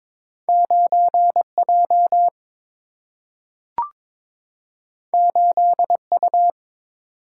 Morse code, heard as a keyed tone, about 22 wpm, 700 Hz. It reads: 9 J 8 U
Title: Single-Letter Single-Number 22wpm